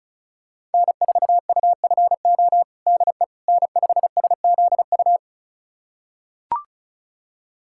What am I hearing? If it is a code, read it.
Morse code, heard as a keyed tone, about 34 wpm, 700 Hz. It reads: N4UFO DE N5SZU
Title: USA Callsign DE Callsign - 5 Characters - 35wpm